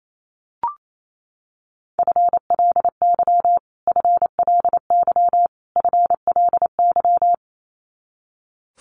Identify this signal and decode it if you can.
Morse code, heard as a keyed tone, about 28 wpm, 700 Hz. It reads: FLY FLY FLY
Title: Top 500 Words Repeated 28wpm